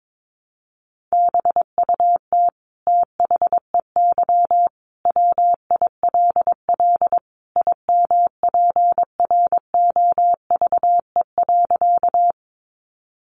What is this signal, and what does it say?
Morse code, heard as a keyed tone, about 22 wpm, 700 Hz. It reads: BUT THEY WILL IMPROVE.
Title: Sentences from Top 800 Words 22wpm